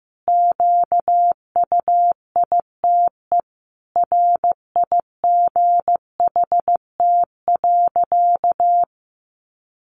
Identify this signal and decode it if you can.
Morse code, heard as a keyed tone, about 15 wpm, 700 Hz. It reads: QUITE RIGHT.
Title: Sentences from Top 700 Words 15wpm